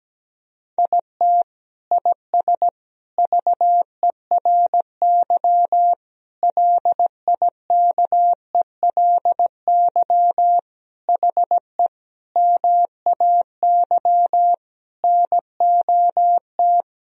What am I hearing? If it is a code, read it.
Morse code, heard as a keyed tone, about 17 wpm, 700 Hz. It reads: IT IS VERY LIKELY HE MAY NOT
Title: Sentences from Top 1000 Words - Review Part 4 - 17wpm